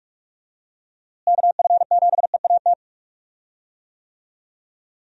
Morse code, heard as a keyed tone, about 47 wpm, 700 Hz. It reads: KF7EAT
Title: USA Call Signs - 6 Characters - Repeated 3x - 45wpm